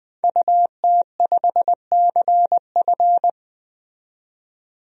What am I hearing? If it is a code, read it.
Morse code, heard as a keyed tone, about 20 wpm, 700 Hz. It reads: UT5CF
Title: DXCC-100 Call Signs - 5 Characters - 20wpm